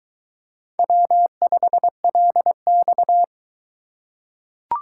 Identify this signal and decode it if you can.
Morse code, heard as a keyed tone, about 22 wpm, 700 Hz. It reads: W5LX